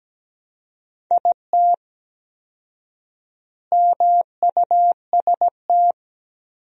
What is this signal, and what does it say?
Morse code, heard as a keyed tone, about 17 wpm, 700 Hz. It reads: IT MUST
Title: 4x Word-Spacing - Sentences from Top 500 Words - 17wpm